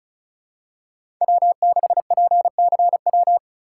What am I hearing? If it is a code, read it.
Morse code, heard as a keyed tone, about 36 wpm, 700 Hz. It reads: W6PCW
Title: USA Call Signs - 3x Character Spacing - 35wpm